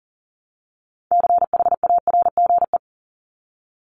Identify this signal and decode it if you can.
Morse code, heard as a keyed tone, about 40 wpm, 700 Hz. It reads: CHARGE
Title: Top 751-1000 Words 40wpm